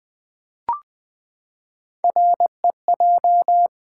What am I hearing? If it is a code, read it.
Morse code, heard as a keyed tone, about 20 wpm, 700 Hz. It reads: REJ
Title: Build A Word - 3 Letters At A Time - 6-Letter Words - 20wpm